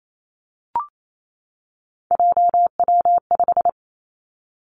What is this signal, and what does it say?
Morse code, heard as a keyed tone, about 28 wpm, 700 Hz. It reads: JW5
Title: Three Challenging Letter-Number Combinations 28wpm